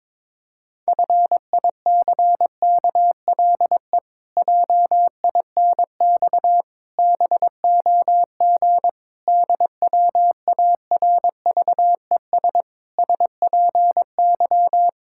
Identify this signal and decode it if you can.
Morse code, heard as a keyed tone, about 22 wpm, 700 Hz. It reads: FICKLE JINX BOG DWARVES SPY
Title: Pangram Challenge v2 - 22wpm